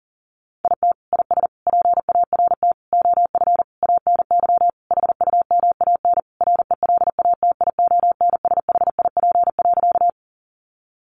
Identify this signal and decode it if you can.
Morse code, heard as a keyed tone, about 40 wpm, 700 Hz. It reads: IT IS PART OF ANY HUMAN RELATIONSHIP.
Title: Sentences from Top 1100 Words 40wpm